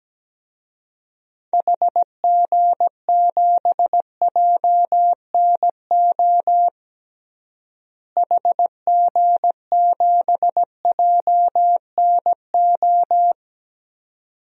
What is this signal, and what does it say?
Morse code, heard as a keyed tone, about 17 wpm, 700 Hz. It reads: HG7JNO HG7JNO
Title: DXCC-100 Call Signs - 6 Characters - Repeated 2x - 17wpm